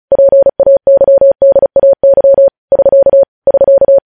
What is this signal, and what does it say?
Morse code, heard as a keyed tone, about 36 wpm, 550 Hz. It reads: PAYDAY <SK> <SK>
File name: podcast / News headlines-1x-35x35